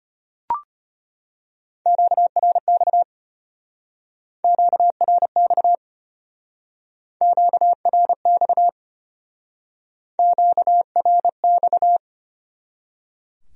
Morse code, heard as a keyed tone, about 33 wpm, 700 Hz. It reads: QRX QRX QRX QRX